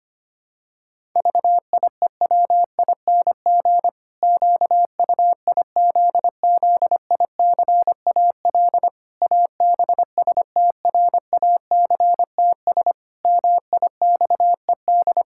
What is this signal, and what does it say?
Morse code, heard as a keyed tone, about 25 wpm, 700 Hz. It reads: VIEWING QUIZZICAL ABSTRACTS MIXED
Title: Pangram Challenge v2 - 25wpm